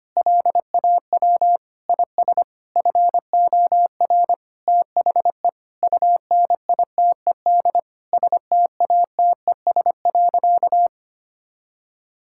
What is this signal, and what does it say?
Morse code, heard as a keyed tone, about 25 wpm, 700 Hz. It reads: LAW IS FOR THE UNITED STATES.
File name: practice / 25wpm / Sentences with 4-Word Phrases - 25wpm